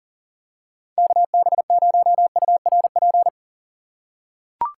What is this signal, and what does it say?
Morse code, heard as a keyed tone, about 39 wpm, 700 Hz. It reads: KB0URP
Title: USA Call Signs - 6 Characters - 2x Character Spacing - 40wpm